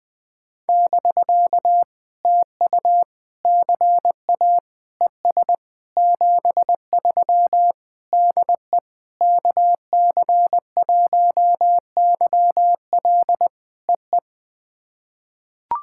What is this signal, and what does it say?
Morse code, heard as a keyed tone, about 20 wpm, 700 Hz. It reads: <BK> TU CA ES 73 DE KC1YL EE
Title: POTA QSO Practice - 20wpm